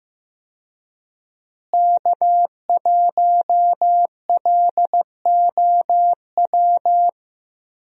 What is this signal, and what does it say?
Morse code, heard as a keyed tone, about 15 wpm, 700 Hz. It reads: K1LOW
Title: USA Call Signs - 5 Characters - Repeated 3x - 15wpm